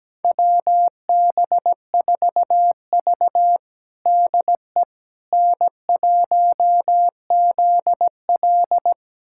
Morse code, heard as a keyed tone, about 17 wpm, 700 Hz. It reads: WB4V DE N1ZL
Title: USA Callsign DE Callsign - 4 Characters - 17wpm